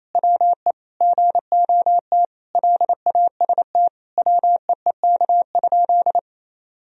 Morse code, heard as a keyed tone, about 28 wpm, 700 Hz. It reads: WE GOT LAST WEEK?
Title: Sentences from Top 1000 Words - Review Part 3 - 28wpm